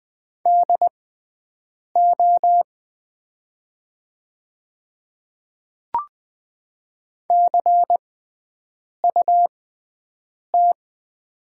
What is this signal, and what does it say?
Morse code, heard as a keyed tone, about 20 wpm, spaced as for 7 wpm, 700 Hz. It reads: DO CUT
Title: 6x Character Spacing - Top 300 Words - 20wpm